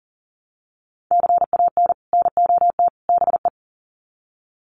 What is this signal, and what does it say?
Morse code, heard as a keyed tone, about 40 wpm, 700 Hz. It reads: CAN NOT BE
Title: Sets of 3 Words 40wpm